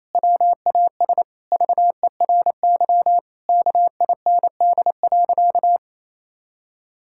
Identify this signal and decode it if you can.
Morse code, heard as a keyed tone, about 27 wpm, 700 Hz. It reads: WAS VERY KIND.
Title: Sentences from Top 1600 Words 28wpm